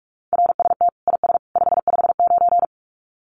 Morse code, heard as a keyed tone, about 44 wpm, 700 Hz. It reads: RST IS 559
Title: Common QSO Phrases 45wpm